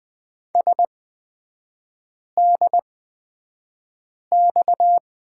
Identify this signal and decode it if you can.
Morse code, heard as a keyed tone, about 20 wpm, spaced as for 5 wpm, 690 Hz. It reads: SDX
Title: 236 Lesson 38e - Letter q - Three Characters 20wpm